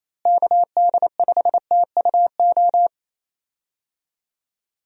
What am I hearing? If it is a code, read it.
Morse code, heard as a keyed tone, about 27 wpm, 700 Hz. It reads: KD5TUO
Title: USA Call Signs - 6 Characters - Repeated 2x - 28wpm